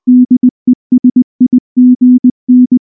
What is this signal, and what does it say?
Morse code, heard as a keyed tone, about 20 wpm, 250 Hz. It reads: DESIGN
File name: morse / morse-training-20WPM-6